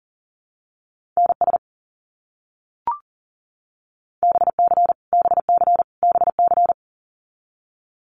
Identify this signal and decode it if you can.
Morse code, heard as a keyed tone, about 40 wpm, 700 Hz. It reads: NS BC BC BC
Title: Canadian Provinces and Territories - Repeated 3x - 40wpm